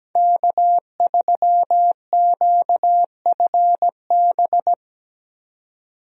Morse code, heard as a keyed tone, about 17 wpm, 700 Hz. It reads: K3QFB
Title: USA Call Signs - 5 Characters - 17wpm